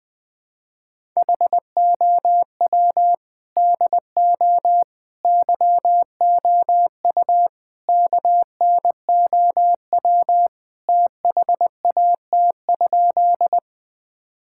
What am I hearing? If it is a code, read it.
Morse code, heard as a keyed tone, about 20 wpm, 700 Hz. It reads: HOW DO YOU KNOW THAT?
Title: Sentences with 4-Word Phrases - Spoken Beginning - 20wpm